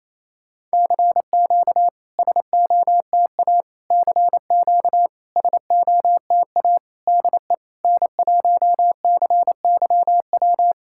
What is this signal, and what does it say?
Morse code, heard as a keyed tone, about 28 wpm, 700 Hz. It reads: CQ SOTA CQ SOTA DE N1CYW
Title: SOTA QSO Practice 28wpm